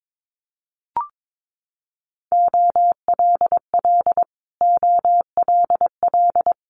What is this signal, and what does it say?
Morse code, heard as a keyed tone, about 22 wpm, 700 Hz. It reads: OLL OLL
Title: Multi-letter Phonemes Repeated 3x 22wpm